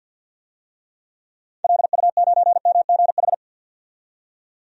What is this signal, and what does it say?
Morse code, heard as a keyed tone, about 47 wpm, 700 Hz. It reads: LU9MGH
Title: DXCC-100 Call Signs - 6 Characters - 50wpm